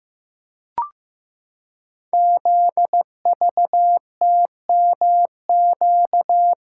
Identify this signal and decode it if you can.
Morse code, heard as a keyed tone, about 15 wpm, 700 Hz. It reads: ZVTMQ
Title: Code Groups - 5 Letters - 15wpm